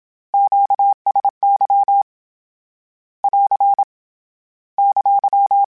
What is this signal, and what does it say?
Morse code, heard as a keyed tone, about 27 wpm, 800 Hz. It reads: QSY <AR> !
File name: Q Calls / Q Calls-mp3s / Q Calls Once @25